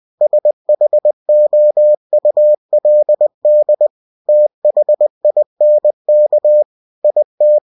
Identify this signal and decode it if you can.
Morse code, heard as a keyed tone, about 20 wpm, 600 Hz. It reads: SHOULD THINK IT
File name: cw-books / winnie-the-pooh / winnie-20wpm-page-0085